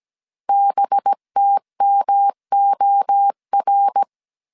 Morse code, heard as a keyed tone, about 17 wpm, 800 Hz. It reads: BTMOR
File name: CW / DET / 1 6 1500C S A 16wpm 2635